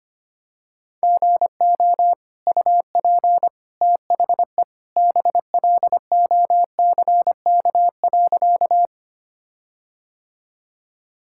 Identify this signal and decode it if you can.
Morse code, heard as a keyed tone, about 25 wpm, 700 Hz. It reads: GO UP THE BLOCK.
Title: Sentences from Top 1600 Words 25wpm